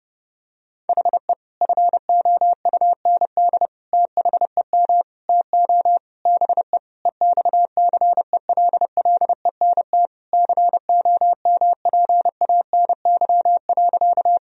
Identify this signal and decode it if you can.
Morse code, heard as a keyed tone, about 30 wpm, 700 Hz. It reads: HE FOUND THEM TO BE EXCELLENT COMPANY.
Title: Sentences from Top 1000 Words - Review Part 3 - 30wpm